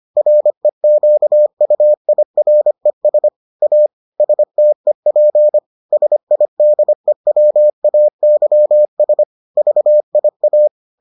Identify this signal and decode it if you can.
Morse code, heard as a keyed tone, about 25 wpm, 600 Hz. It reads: REQUIRES A STEP SIDEWAYS VIA